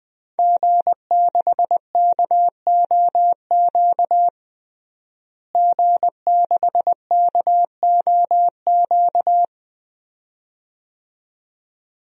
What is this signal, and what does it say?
Morse code, heard as a keyed tone, about 21 wpm, 700 Hz. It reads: G6KOQ G6KOQ
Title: DXCC-100 Call Signs - 5 Characters - Repeated 2x - 20wpm